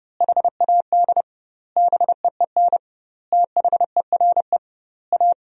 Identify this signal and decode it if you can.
Morse code, heard as a keyed tone, about 30 wpm, 700 Hz. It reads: HAD BEEN THERE A
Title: 2x Word-Spacing - Sentences from Top 400 Words - 30wpm